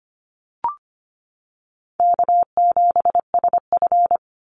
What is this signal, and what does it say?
Morse code, heard as a keyed tone, about 25 wpm, 700 Hz. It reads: K7SF
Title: Easy Call Signs Repeated 3x - 25wpm